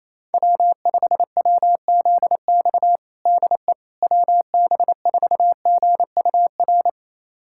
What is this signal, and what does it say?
Morse code, heard as a keyed tone, about 28 wpm, 700 Hz. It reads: W5WZX DE WB4GUR
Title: USA Callsign DE Callsign 28wpm